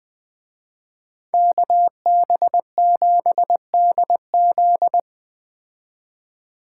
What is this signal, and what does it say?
Morse code, heard as a keyed tone, about 20 wpm, 700 Hz. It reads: KB7DZ